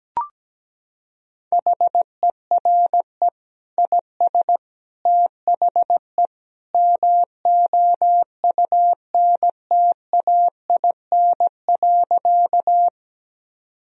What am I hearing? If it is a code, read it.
Morse code, heard as a keyed tone, about 17 wpm, 700 Hz. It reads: HERE IS THE MOUNTAIN.
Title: Sentences from Top 1000 Words - Review Part 1 - 17wpm